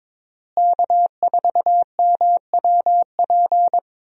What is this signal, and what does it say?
Morse code, heard as a keyed tone, about 22 wpm, 700 Hz. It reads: K4MWP
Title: USA Call Signs - 5 Characters - Repeated 3x - 22wpm